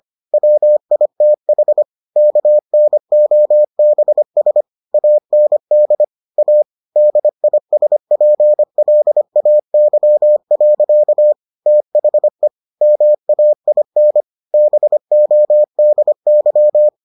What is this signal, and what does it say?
Morse code, heard as a keyed tone, about 25 wpm, 600 Hz. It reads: WITH KNOBS AND A DISPLAY. THE MAIN BODY